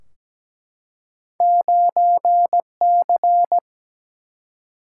Morse code, heard as a keyed tone, about 17 wpm, 700 Hz. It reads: 9C
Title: Single-Letter Single-Number 17wpm